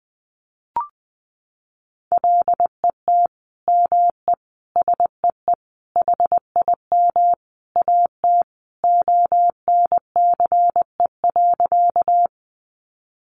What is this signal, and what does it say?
Morse code, heard as a keyed tone, about 20 wpm, 700 Hz. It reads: LET ME SEE HIM AT ONCE.
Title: Sentences from Top 300 Words 20wpm